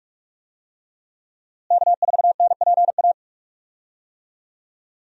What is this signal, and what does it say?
Morse code, heard as a keyed tone, about 46 wpm, 700 Hz. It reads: K4NPA
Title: USA Call Signs - 5 Characters - Repeated 3x - 45wpm